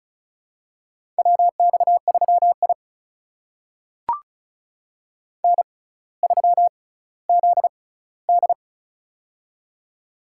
Morse code, heard as a keyed tone, about 36 wpm, spaced as for 18 wpm, 700 Hz. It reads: WX3I N3ZD
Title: USA Call Signs - 4 Characters - 6x Character Spacing - 35wpm